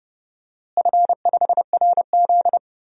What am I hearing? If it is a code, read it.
Morse code, heard as a keyed tone, about 30 wpm, 700 Hz. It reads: F5RZ
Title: DXCC-100 Call Signs - 4 Characters - Repeated 3x - 30wpm